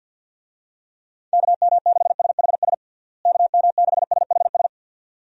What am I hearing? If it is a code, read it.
Morse code, heard as a keyed tone, about 52 wpm, 700 Hz. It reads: KM6SHS KM6SHS
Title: USA Call Signs - 6 Characters - Repeated 2x - 50wpm